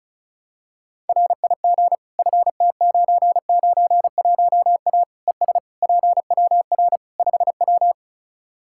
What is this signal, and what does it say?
Morse code, heard as a keyed tone, about 36 wpm, 700 Hz. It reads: RIG FT991A ES PWR 5W
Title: Practice QSO - Exchange 2 - Info - 35wpm